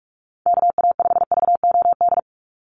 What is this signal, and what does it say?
Morse code, heard as a keyed tone, about 44 wpm, 700 Hz. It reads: KA5VOD
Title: USA Call Signs 45wpm